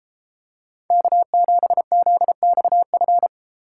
Morse code, heard as a keyed tone, about 30 wpm, 700 Hz. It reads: K7ZXF